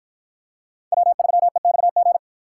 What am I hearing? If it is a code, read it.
Morse code, heard as a keyed tone, about 51 wpm, 700 Hz. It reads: W3EXG